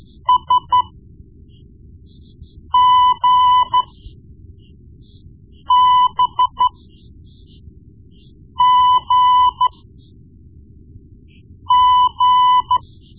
Morse code, heard as a keyed tone, about 13 wpm, spaced as for 4 wpm, 1000 Hz. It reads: SGBGG
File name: Morse Lessons / Morse Lessons / Tape-1B